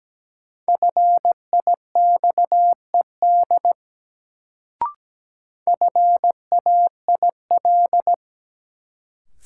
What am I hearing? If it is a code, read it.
Morse code, heard as a keyed tone, about 17 wpm, 700 Hz. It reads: FIXED FAIL
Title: Sentences from Top 1100 Words - New Word Drill - 17wpm